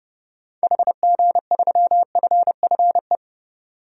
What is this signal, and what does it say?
Morse code, heard as a keyed tone, about 31 wpm, 700 Hz. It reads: HG3FFE
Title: DXCC-100 Call Signs - 6 Characters - 30wpm